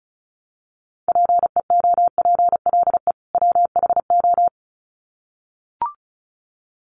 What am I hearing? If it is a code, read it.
Morse code, heard as a keyed tone, about 34 wpm, 700 Hz. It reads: PEOPLE WHO
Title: Sets of 2 Words 35wpm